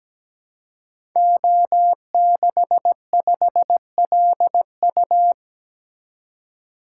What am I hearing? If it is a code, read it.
Morse code, heard as a keyed tone, about 17 wpm, 700 Hz. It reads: O65LU
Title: Code Groups - 5 Letter:Numbers - 17wpm